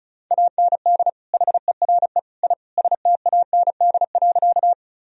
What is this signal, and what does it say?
Morse code, heard as a keyed tone, about 34 wpm, 700 Hz. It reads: AND HERE I STAND.
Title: Sentences from Top 1000 Words - Review Part 1 - 35wpm